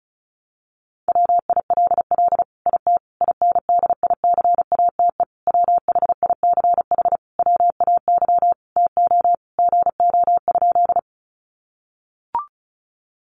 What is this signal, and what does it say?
Morse code, heard as a keyed tone, about 34 wpm, 700 Hz. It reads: WILL IT INDICATE WHICH WAY TO GO?
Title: Sentences from Top 1300 Words 35wpm